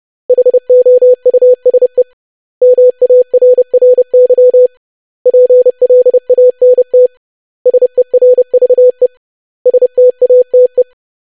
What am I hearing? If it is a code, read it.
Morse code, heard as a keyed tone, about 30 wpm, 500 Hz. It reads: HOUSE MARRY PLANT SERVE STATE
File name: lesson / web / basic500-25wpm